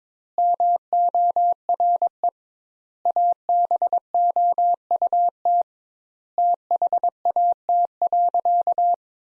Morse code, heard as a keyed tone, about 22 wpm, 700 Hz. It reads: MORE ABOUT THAT.
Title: Sentences Top 200 Words - 22wpm